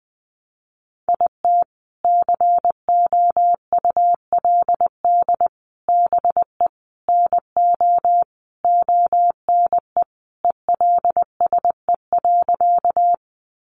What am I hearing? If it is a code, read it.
Morse code, heard as a keyed tone, about 20 wpm, 700 Hz. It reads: IT COULD BE NO ONE ELSE.
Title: Sentences from Top 800 Words 20wpm